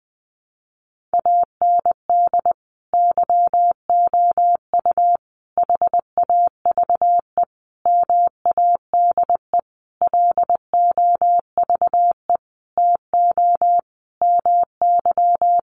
Morse code, heard as a keyed tone, about 20 wpm, 700 Hz. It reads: AND YOU HAVE MADE LOVE TO MY
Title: Sentences from Top 1500 Words 20wpm